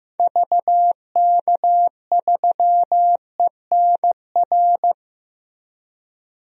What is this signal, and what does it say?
Morse code, heard as a keyed tone, about 16 wpm, 700 Hz. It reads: VK3ENR
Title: DXCC-100 Call Signs - 6 Characters - Repeated 3x - 15wpm